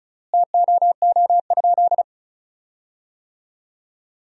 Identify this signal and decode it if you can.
Morse code, heard as a keyed tone, about 34 wpm, 700 Hz. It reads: TOO?
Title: 3x Word-Spacing - Sentences from Top 200 Words - 35wpm